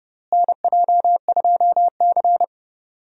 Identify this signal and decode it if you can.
Morse code, heard as a keyed tone, about 32 wpm, 700 Hz. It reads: NJ2C
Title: USA Call Signs - 4 Characters - Repeated 3x - 30wpm